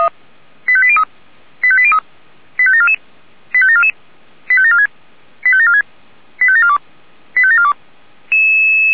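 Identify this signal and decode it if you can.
Five-tone selective call (ZVEI), heard as quick runs of five tones, tones of about 70 ms. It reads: 85803 85803 85744 85744 85746 85746 85742 85742
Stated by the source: Five-tone ZVEI selective call received on SDR.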